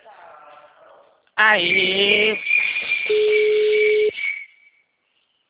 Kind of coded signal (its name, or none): telephone call progress tone